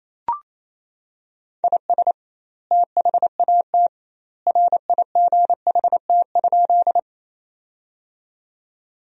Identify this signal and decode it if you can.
Morse code, heard as a keyed tone, about 28 wpm, 700 Hz. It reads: IS THAT RIGHT?
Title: 2x Word-Spacing - Sentences from Top 200 Words - 28wpm